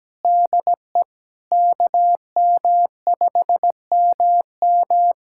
Morse code, heard as a keyed tone, about 17 wpm, 700 Hz. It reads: DE KM5MM
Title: USA Callsign DE Callsign 17wpm